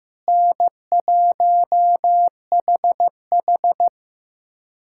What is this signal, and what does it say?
Morse code, heard as a keyed tone, about 15 wpm, 700 Hz. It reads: N1HH
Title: USA Callsign DE Callsign - 4 Characters - 15wpm